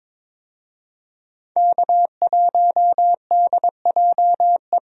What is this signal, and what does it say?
Morse code, heard as a keyed tone, about 22 wpm, 700 Hz. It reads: K1DJE
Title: USA Call Signs - 5 Characters - Repeated 3x - 22wpm